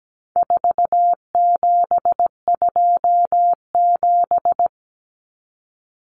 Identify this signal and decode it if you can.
Morse code, heard as a keyed tone, about 17 wpm, 700 Hz. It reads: 4727
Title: Numbers - 4-Digit 17wpm